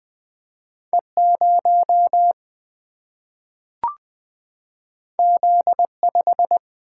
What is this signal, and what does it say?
Morse code, heard as a keyed tone, about 20 wpm, 700 Hz. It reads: E0 Z5
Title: Single-Letter Single-Number 20wpm